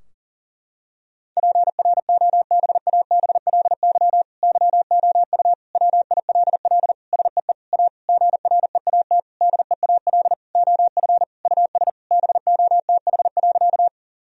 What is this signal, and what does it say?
Morse code, heard as a keyed tone, about 37 wpm, 700 Hz. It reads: PROBABLY YOU WILL SEE A GREAT DEAL OF US BOTH.
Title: Sentences from Top 1000 Words - Review Part 4 - 40wpm